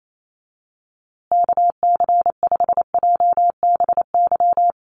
Morse code, heard as a keyed tone, about 28 wpm, 700 Hz. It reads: KC5JBY
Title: USA Call Signs Repeated 3x 28wpm